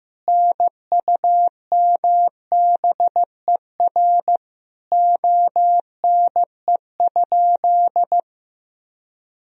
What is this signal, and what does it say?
Morse code, heard as a keyed tone, about 15 wpm, 700 Hz. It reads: NUMBER ONE?
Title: Sentences from Top 1000 Words - Review Part 1 - 15wpm